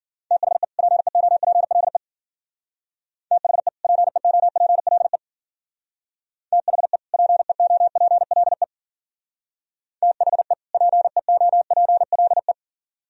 Morse code, heard as a keyed tone, about 52 wpm, 700 Hz. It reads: THE PEOPLE THE PEOPLE THE PEOPLE THE PEOPLE